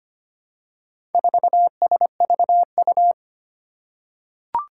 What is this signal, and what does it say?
Morse code, heard as a keyed tone, about 25 wpm, 700 Hz. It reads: 4SVU
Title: Four Challenging Letter-Number Combinations 25wpm